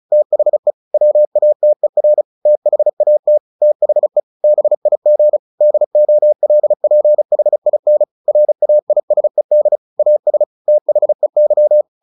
Morse code, heard as a keyed tone, about 34 wpm, 600 Hz. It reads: THE WATER THAT THE BIG DOLPHIN RAISED AS THEY
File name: the old man and the sea / the old man and the sea 1 35wpm